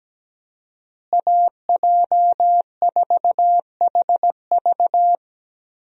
Morse code, heard as a keyed tone, about 17 wpm, 700 Hz. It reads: AJ4HV
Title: USA Call Signs - 5 Characters - Repeated 3x - 17wpm